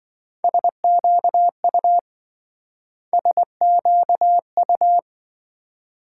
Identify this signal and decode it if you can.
Morse code, heard as a keyed tone, about 26 wpm, 700 Hz. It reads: SQU SQU